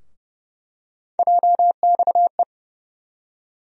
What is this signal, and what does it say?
Morse code, heard as a keyed tone, about 30 wpm, 700 Hz. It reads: JXE
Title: USA Call Sign Postfixes - 3 Letters 30wpm